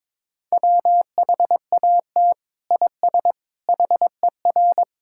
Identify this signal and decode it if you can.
Morse code, heard as a keyed tone, about 21 wpm, 700 Hz. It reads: WHAT IS HER
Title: Sentences from Top 1000 Words - Review Part 6 - 22wpm